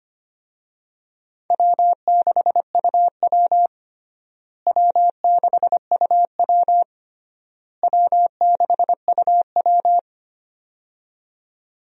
Morse code, heard as a keyed tone, about 25 wpm, 700 Hz. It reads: W6UW W6UW W6UW
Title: USA Call Signs - 4 Characters - Repeated 3x - 25wpm